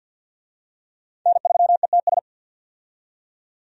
Morse code, heard as a keyed tone, about 50 wpm, 700 Hz. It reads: N3ETS
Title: USA Call Signs - 5 Characters - 50wpm